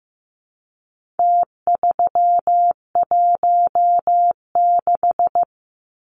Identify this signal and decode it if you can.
Morse code, heard as a keyed tone, about 15 wpm, 700 Hz. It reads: T316
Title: Numbers - Contest Style - 4-Digit 15wpm